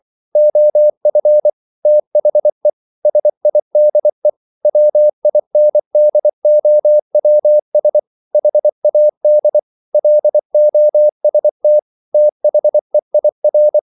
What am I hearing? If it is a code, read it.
Morse code, heard as a keyed tone, about 24 wpm, 600 Hz. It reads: OF THE SIDE WINDOWS HAD LOST THEIR